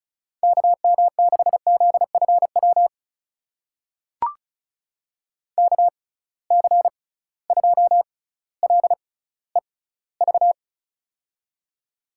Morse code, heard as a keyed tone, about 36 wpm, spaced as for 18 wpm, 700 Hz. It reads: KM6ZFW KC2LEV
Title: USA Call Signs - 6 Characters - 6x Character Spacing - 35wpm